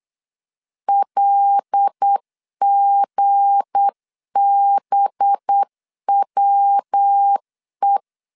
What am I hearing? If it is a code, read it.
Morse code, heard as a keyed tone, about 8 wpm, 800 Hz. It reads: LGBWE
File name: LEZIONI / 7 GF9Z / 7 G 8wpm 3118